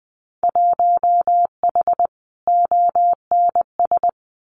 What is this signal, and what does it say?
Morse code, heard as a keyed tone, about 19 wpm, 700 Hz. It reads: 1H ONS
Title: Winter Field Day Exchanges 20wpm